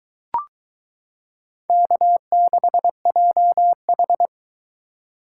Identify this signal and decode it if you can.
Morse code, heard as a keyed tone, about 22 wpm, 700 Hz. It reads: K6JH